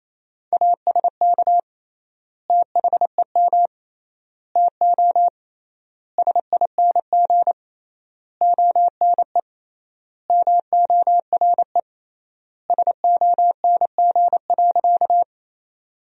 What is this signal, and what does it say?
Morse code, heard as a keyed tone, about 28 wpm, 700 Hz. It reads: ASK THEM TO SING ONE MORE SONG.
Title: 3x Word-Spacing - Sentences from Top 400 Words - 28wpm